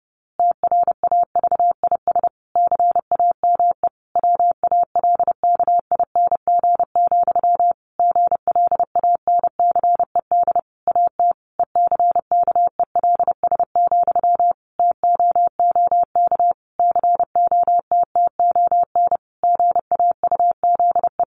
Morse code, heard as a keyed tone, about 30 wpm, 700 Hz. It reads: TRAVIS CAME WALKING, GLANCED AT ECKELS, TOOK COTTON GAUZE
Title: A Sound Of Thunder 30wpm